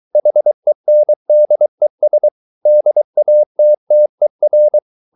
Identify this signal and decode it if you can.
Morse code, heard as a keyed tone, about 23 wpm, 600 Hz. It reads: HENDES DATTER
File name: Ferie QRS 23wpm